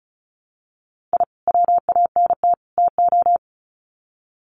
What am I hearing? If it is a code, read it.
Morse code, heard as a keyed tone, about 34 wpm, 700 Hz. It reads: I WANT TO
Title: Sets of 3 Words 35wpm